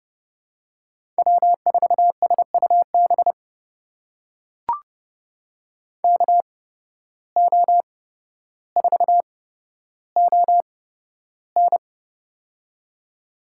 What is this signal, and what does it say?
Morse code, heard as a keyed tone, about 30 wpm, spaced as for 13 wpm, 700 Hz. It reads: W4SUB KO4ON
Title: USA Call Signs - 5 Characters - 8x Character Spacing - 30wpm